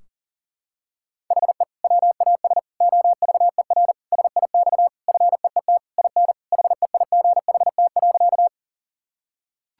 Morse code, heard as a keyed tone, about 40 wpm, 700 Hz. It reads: HE WAS OVER SIX FEET IN HEIGHT.
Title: Sentences from Top 1600 Words 40wpm